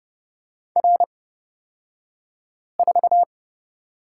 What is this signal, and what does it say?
Morse code, heard as a keyed tone, about 30 wpm, spaced as for 5 wpm, 700 Hz. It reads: R4
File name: Lesson 37d - Letter x - Two Characters 30wpm